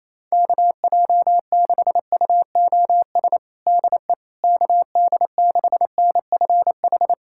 Morse code, heard as a keyed tone, about 28 wpm, 700 Hz. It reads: KJ6UOS DE KD6NFH
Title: USA Callsign DE Callsign - 6 Characters - 28wpm